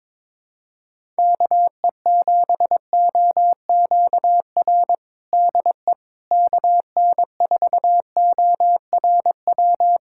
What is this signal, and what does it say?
Morse code, heard as a keyed tone, about 21 wpm, 700 Hz. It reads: KE7OQR DE KN4ORW
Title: USA Callsign DE Callsign - 6 Characters - 22wpm